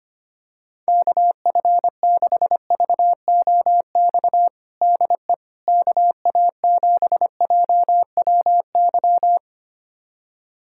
Morse code, heard as a keyed tone, about 25 wpm, 700 Hz. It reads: KF6VOX DE KA7JWY
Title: USA Callsign DE Callsign - 6 Characters - 25wpm